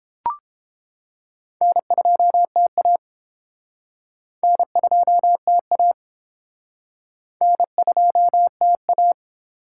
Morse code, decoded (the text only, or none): N2TA N2TA N2TA